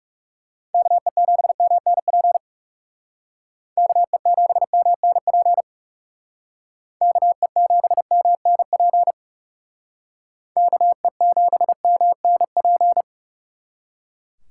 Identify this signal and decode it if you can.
Morse code, heard as a keyed tone, about 38 wpm, 700 Hz. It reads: KE7MNP KE7MNP KE7MNP KE7MNP